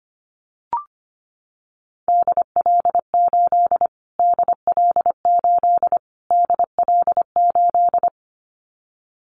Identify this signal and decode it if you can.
Morse code, heard as a keyed tone, about 25 wpm, 700 Hz. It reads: DL8 DL8 DL8
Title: Callsign Prefixes for Top 25 DXCC - Repeated 3x - 25wpm